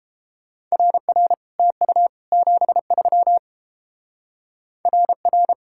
Morse code, heard as a keyed tone, about 31 wpm, 700 Hz. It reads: RR TU 73 RR